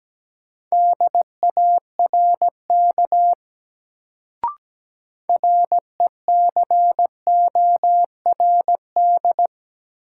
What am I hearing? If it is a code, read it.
Morse code, heard as a keyed tone, about 17 wpm, 700 Hz. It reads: DARK RECORD
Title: Sentences from Top 500 Words - New Word Drill - 17wpm